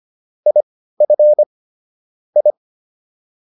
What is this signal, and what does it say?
Morse code, heard as a keyed tone, about 25 wpm, spaced as for 15 wpm, 600 Hz. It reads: IF I